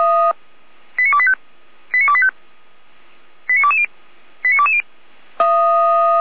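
Five-tone selective call (ZVEI), heard as quick runs of five tones, tones of about 70 ms. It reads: 89285 89285 89220 89220
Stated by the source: Five-tone ZVEI selective call 89223 received by SDR, served by Feuerwehr Orlamünde.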